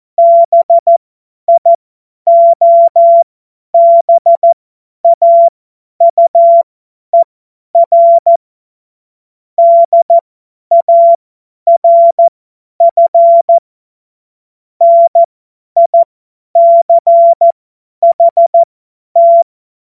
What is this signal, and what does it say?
Morse code, heard as a keyed tone, about 14 wpm, spaced as for 10 wpm, 680 Hz. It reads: BIOBAUER DARF NICHT
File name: Morsekurs / Witze / Witz 27. April 2021, 50 F70